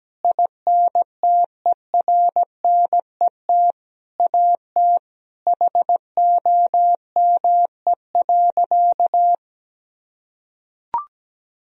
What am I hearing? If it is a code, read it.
Morse code, heard as a keyed tone, about 17 wpm, 700 Hz. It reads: INTERNET AT HOME.
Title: Sentences from Top 1000 Words - Review Part 4 - 17wpm